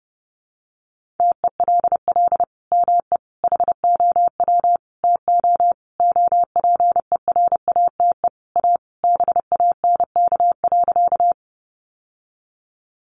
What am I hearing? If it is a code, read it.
Morse code, heard as a keyed tone, about 30 wpm, 700 Hz. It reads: TELL ME HOW TO OPERATE A BANK.
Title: Sentences from Top 1500 Words 30wpm